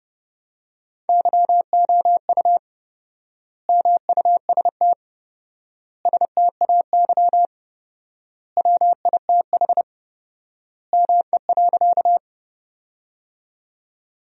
Morse code, heard as a keyed tone, about 32 wpm, 700 Hz. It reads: YOU MUST STAY WITH ME.
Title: 4x Word-Spacing - Sentences from Top 500 Words - 30wpm